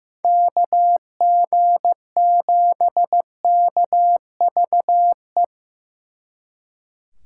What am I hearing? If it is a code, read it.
Morse code, heard as a keyed tone, about 18 wpm, 700 Hz. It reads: KG7KVE
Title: Speed Racing - USA 2x3 Call Signs 15wpm